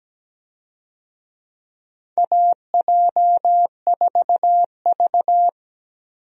Morse code, heard as a keyed tone, about 17 wpm, 700 Hz. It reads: AJ4V
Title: USA Call Signs - 4 Characters - Repeated 3x - 17wpm